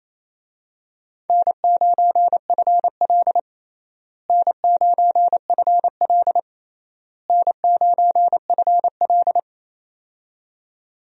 Morse code, heard as a keyed tone, about 28 wpm, 700 Hz. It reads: N9FL N9FL N9FL
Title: USA Call Signs - 4 Characters - Repeated 3x - 28wpm